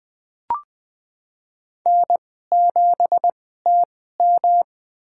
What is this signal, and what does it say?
Morse code, heard as a keyed tone, about 20 wpm, spaced as for 15 wpm, 700 Hz. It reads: N7TM
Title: USA Call Signs - 4 Characters - 2x Character Spacing - 20wpm